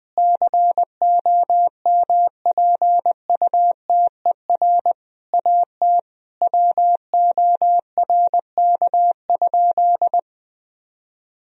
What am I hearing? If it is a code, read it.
Morse code, heard as a keyed tone, about 20 wpm, 700 Hz. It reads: COMPUTER AT WORK?
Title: Sentences from Top 1000 Words - Review Part 3 - 20wpm